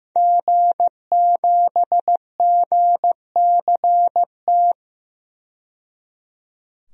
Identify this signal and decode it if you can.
Morse code, heard as a keyed tone, about 15 wpm, 700 Hz. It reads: G7GCT
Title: DXCC-100 Call Signs - 5 Characters - 15wpm